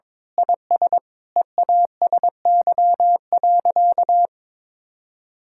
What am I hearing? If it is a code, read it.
Morse code, heard as a keyed tone, about 22 wpm, 700 Hz. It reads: IS EASY.
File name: Sentences with 7-Word Phrases - 22wpm